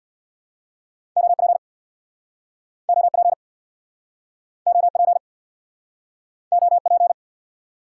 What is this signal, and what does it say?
Morse code, heard as a keyed tone, about 64 wpm, 700 Hz. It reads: OP OP OP OP